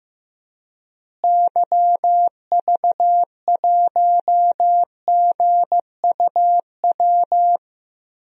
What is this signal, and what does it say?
Morse code, heard as a keyed tone, about 15 wpm, 700 Hz. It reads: YV1GUW
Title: DXCC-100 Call Signs - 6 Characters - Repeated 3x - 15wpm